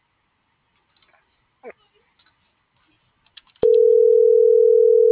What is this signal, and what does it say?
Ringback tone (North America), 440 + 480 Hz.
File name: CENTRAL TERRESTRE / out jessamynvillafuerte